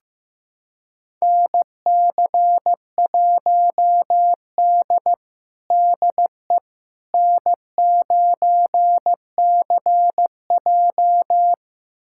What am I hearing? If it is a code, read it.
Morse code, heard as a keyed tone, about 15 wpm, 700 Hz. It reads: NC1D DE N9CJ
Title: USA Callsign DE Callsign - 4 Characters - 15wpm